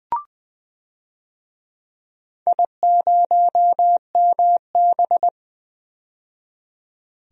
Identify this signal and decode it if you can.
Morse code, heard as a keyed tone, about 20 wpm, 700 Hz. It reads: I0MB
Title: DXCC-100 Call Signs - 4 Characters - 20wpm